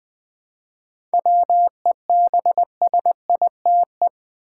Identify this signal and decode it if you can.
Morse code, heard as a keyed tone, about 20 wpm, 700 Hz. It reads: WEBSITE
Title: 7 Letter Words - First 3 Letters Given - 20wpm